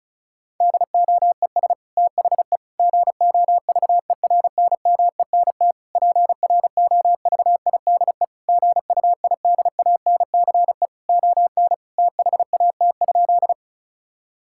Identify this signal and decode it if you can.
Morse code, heard as a keyed tone, about 34 wpm, 700 Hz. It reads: DOES THE GOVERNMENT PROVIDE GUIDANCE ON THAT?
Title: Sentences from Top 1600 Words 35wpm